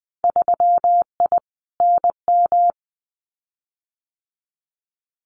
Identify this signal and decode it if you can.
Morse code, heard as a keyed tone, about 20 wpm, 700 Hz. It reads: 3I NM
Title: Winter Field Day Exchanges Encore 20wpm